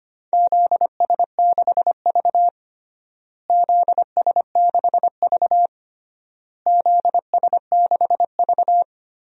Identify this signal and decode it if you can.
Morse code, heard as a keyed tone, about 25 wpm, 700 Hz. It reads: ZS6V ZS6V ZS6V
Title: DXCC-100 Call Signs - 4 Characters - Repeated 3x - 25wpm